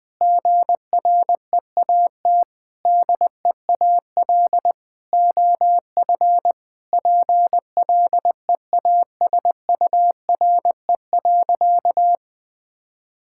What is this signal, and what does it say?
Morse code, heard as a keyed tone, about 20 wpm, 700 Hz. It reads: GREAT DEAL OF PLEASURE.
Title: Sentences from Top 1600 Words 20wpm